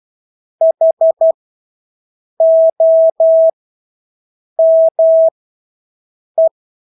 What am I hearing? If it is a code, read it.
Morse code, heard as a keyed tone, about 12 wpm, spaced as for 6 wpm, 650 Hz. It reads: HOME
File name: Engelske ord 12-6 wpm